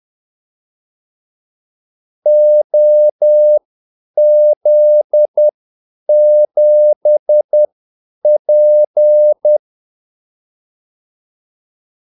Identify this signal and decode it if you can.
Morse code, heard as a keyed tone, about 10 wpm, spaced as for 8 wpm, 600 Hz. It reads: OZ7P
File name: Callsigns-10-8wpm